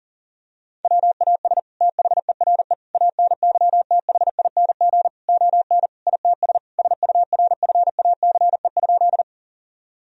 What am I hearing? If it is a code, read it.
Morse code, heard as a keyed tone, about 40 wpm, 700 Hz. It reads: WAS THERE ANYTHING ON ITS SURFACE?
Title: Sentences from Top 1000 Words - Review Part 3 - 40wpm